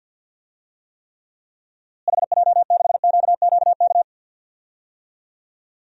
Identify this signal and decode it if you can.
Morse code, heard as a keyed tone, about 52 wpm, 700 Hz. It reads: HJ6QQK
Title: DXCC-100 Call Signs - 6 Characters - Repeated 3x - 50wpm